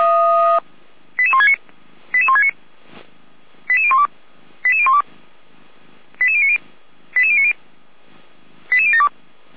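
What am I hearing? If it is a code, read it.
Five-tone selective call (ZVEI), heard as quick runs of five tones, tones of about 70 ms. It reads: 88169 88169 88012 88012 88090 88090 88072
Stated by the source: ZVEI five-tone call 88102 received by SDR.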